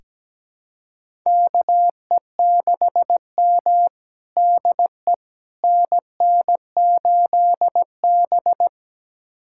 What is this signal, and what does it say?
Morse code, heard as a keyed tone, about 17 wpm, 700 Hz. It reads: KE6M DE NN8B
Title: USA Callsign DE Callsign - 4 Characters - 17wpm